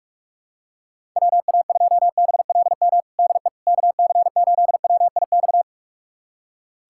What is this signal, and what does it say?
Morse code, heard as a keyed tone, about 45 wpm, 700 Hz. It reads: WA2BLM DE KC8WIX
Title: USA Callsign DE Callsign 45wpm